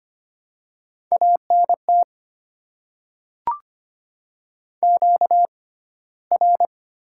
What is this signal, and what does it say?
Morse code, heard as a keyed tone, about 25 wpm, spaced as for 15 wpm, 700 Hz. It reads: ANT QR
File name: Top 20 QSO Elements - 6x Character Spacing - 25wpm